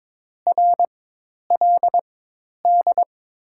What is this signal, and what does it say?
Morse code, heard as a keyed tone, about 22 wpm, spaced as for 10 wpm, 690 Hz. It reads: RLD
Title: 4x Character Spacing - Top 200 Words - 22wpm